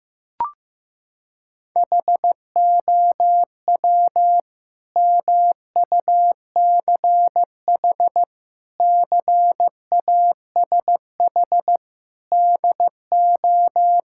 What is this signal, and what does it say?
Morse code, heard as a keyed tone, about 15 wpm, 700 Hz. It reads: HOW MUCH CASH DO
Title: Sentences from Top 1000 Words - Review Part 5 - 15wpm